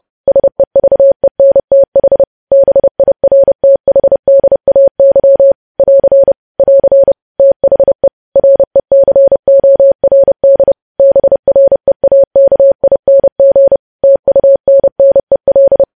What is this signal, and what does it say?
Morse code, heard as a keyed tone, about 30 wpm, 550 Hz. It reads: SEVENTH BIRTHDAY <AR> <AR> THE RECORD BREAKING TUNNEL
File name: podcast / News headlines-1x-30x30